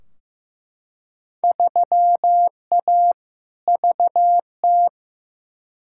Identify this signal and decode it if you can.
Morse code, heard as a keyed tone, about 15 wpm, 700 Hz. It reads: 3A VT
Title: Field Day Exchanges Encore - 15wpm